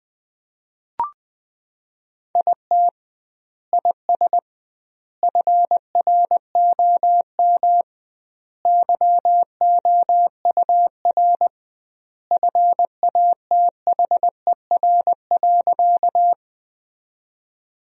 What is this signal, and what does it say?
Morse code, heard as a keyed tone, about 21 wpm, 700 Hz. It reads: IT IS FROM YOUR FATHER.
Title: 2x Word-Spacing - Sentences from Top 300 Words - 20wpm